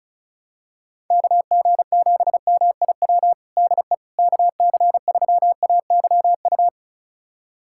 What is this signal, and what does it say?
Morse code, heard as a keyed tone, about 34 wpm, 700 Hz. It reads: KG7MIW DE KC3AYU
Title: USA Callsign DE Callsign 35wpm